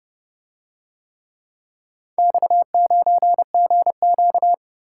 Morse code, heard as a keyed tone, about 30 wpm, 700 Hz. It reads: X9GQ